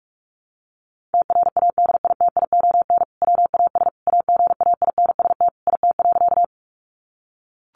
Morse code, heard as a keyed tone, about 44 wpm, 700 Hz. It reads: TRADITION WAS AGAINST IT.
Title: Sentences from Top 1500 Words 45wpm